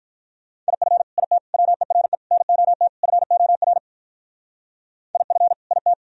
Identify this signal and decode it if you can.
Morse code, heard as a keyed tone, about 45 wpm, 700 Hz. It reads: IF IT WERE NOT FOR IF IT